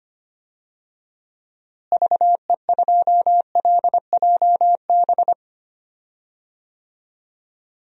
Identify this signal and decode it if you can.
Morse code, heard as a keyed tone, about 25 wpm, 700 Hz. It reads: VE2LJB
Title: DXCC-100 Call Signs - 6 Characters - Repeated 2x - 25wpm